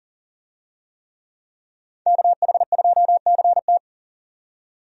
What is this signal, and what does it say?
Morse code, heard as a keyed tone, about 40 wpm, 700 Hz. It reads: KH2CT
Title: USA Call Signs - 5 Characters - 40wpm